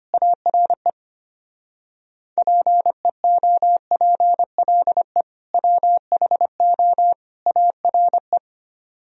Morse code, heard as a keyed tone, about 24 wpm, 700 Hz. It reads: ARE PEOPLE WHO ARE
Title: Speed-Racing - Sets of 3 Words 25wpm